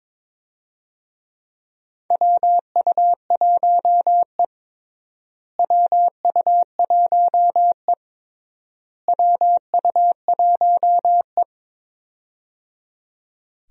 Morse code, heard as a keyed tone, about 22 wpm, 700 Hz. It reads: WU1E WU1E WU1E
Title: USA Call Signs - 4 Characters - Repeated 3x - 22wpm